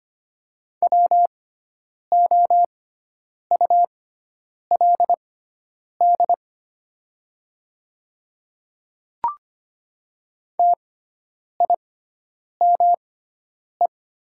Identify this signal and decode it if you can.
Morse code, heard as a keyed tone, about 25 wpm, spaced as for 9 wpm, 700 Hz. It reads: WOULD TIME
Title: 6x Character Spacing - Top 100 Words - 25wpm